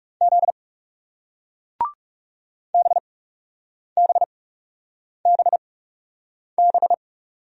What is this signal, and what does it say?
Morse code, heard as a keyed tone, about 40 wpm, spaced as for 8 wpm, 700 Hz. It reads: Z BBBB